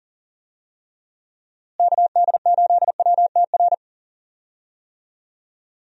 Morse code, heard as a keyed tone, about 39 wpm, 700 Hz. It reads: KD8WTR
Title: USA Call Signs - 6 Characters - Repeated 3x - 40wpm